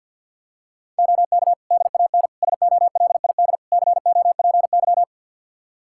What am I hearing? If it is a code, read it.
Morse code, heard as a keyed tone, about 51 wpm, 700 Hz. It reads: OK DAN SOLID COPY
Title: Practice QSO - Exchange 3 - More Info - 50wpm